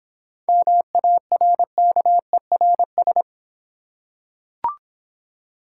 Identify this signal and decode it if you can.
Morse code, heard as a keyed tone, about 26 wpm, 700 Hz. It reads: MARKERS